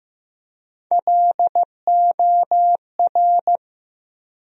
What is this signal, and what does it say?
Morse code, heard as a keyed tone, about 15 wpm, 700 Hz. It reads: LOR
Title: USA Call Sign Postfixes - 3 Letters 15wpm